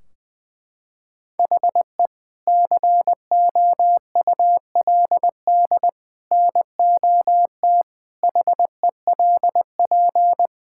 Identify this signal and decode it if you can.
Morse code, heard as a keyed tone, about 20 wpm, 700 Hz. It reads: HE COULD NOT HELP
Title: Sentences from Top 1600 Words 20wpm